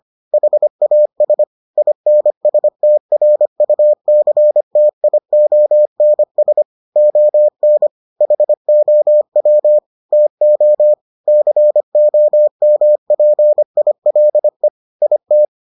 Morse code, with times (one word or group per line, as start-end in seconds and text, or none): 0.3-1.4 HAS
1.8-6.6 INSTRUCTIONS
7.0-7.9 ON
8.2-9.8 HOW
10.1-10.9 TO
11.3-14.7 COMPILE
15.0-15.4 IT